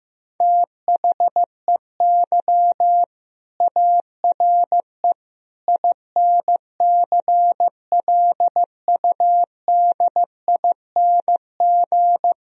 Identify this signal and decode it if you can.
Morse code, heard as a keyed tone, about 16 wpm, 700 Hz. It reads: THEY ARE INCLUDING
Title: Sentences from Top 1000 Words - Review Part 4 - 15wpm